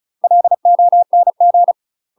Morse code, heard as a keyed tone, about 35 wpm, 700 Hz. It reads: LONG
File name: UK MASTER QSOs TRAINING / 35WPM / UK MASTER QSOs 0014